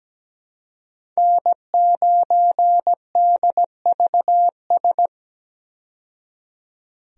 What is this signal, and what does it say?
Morse code, heard as a keyed tone, about 17 wpm, 700 Hz. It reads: N9DVS
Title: USA Call Signs - 5 Characters - 17wpm